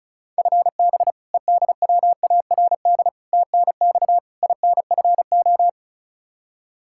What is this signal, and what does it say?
Morse code, heard as a keyed tone, about 34 wpm, 700 Hz. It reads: FB EDWARD TNX INFO
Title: Practice QSO - Exchange 3 - More Info - 35wpm